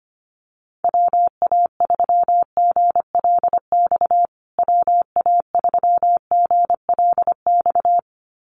Morse code, heard as a keyed tone, about 25 wpm, 700 Hz. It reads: WA3GLX WA3GLX
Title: USA Call Signs Repeated 2x 25wpm